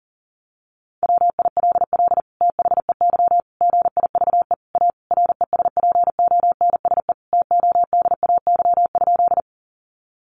Morse code, heard as a keyed tone, about 40 wpm, 700 Hz. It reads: WILL THEY GIVE A RESPONSE TODAY?
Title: Sentences from Top 1400 Words 40wpm